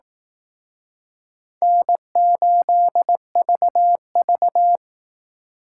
Morse code, heard as a keyed tone, about 19 wpm, 700 Hz. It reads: N8VV